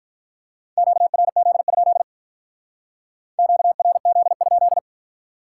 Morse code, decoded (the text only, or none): QRZ? QRZ?